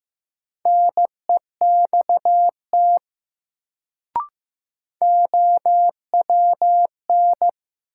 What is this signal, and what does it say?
Morse code, heard as a keyed tone, about 15 wpm, 700 Hz. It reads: NEXT OWN
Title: Sets of 2 Words - New Word Drill - 15wpm